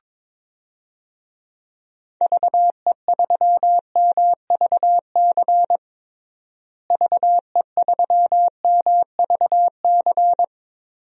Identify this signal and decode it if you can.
Morse code, heard as a keyed tone, about 22 wpm, 700 Hz. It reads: VE3MVC VE3MVC
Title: DXCC-100 Call Signs - 6 Characters - Repeated 3x - 22wpm